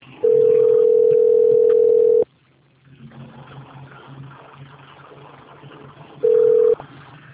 Ringback tone (North America), 440 + 480 Hz.